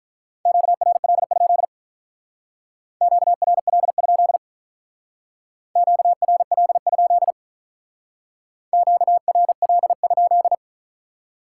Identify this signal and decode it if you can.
Morse code, heard as a keyed tone, about 45 wpm, 700 Hz. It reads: QRL? QRL? QRL? QRL?